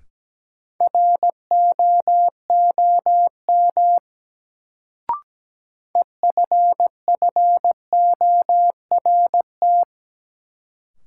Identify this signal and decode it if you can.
Morse code, heard as a keyed tone, about 17 wpm, 700 Hz. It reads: ROOM EFFORT
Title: Sets of 4 Words - New Word Drill - 17wpm